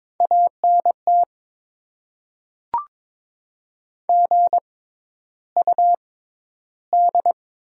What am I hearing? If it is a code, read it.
Morse code, heard as a keyed tone, about 22 wpm, spaced as for 11 wpm, 690 Hz. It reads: ANT GUD
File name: Top 20 QSO Elements - 6x Character Spacing - 22wpm